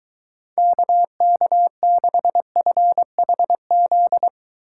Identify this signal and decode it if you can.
Morse code, heard as a keyed tone, about 23 wpm, 700 Hz. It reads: KK6FHZ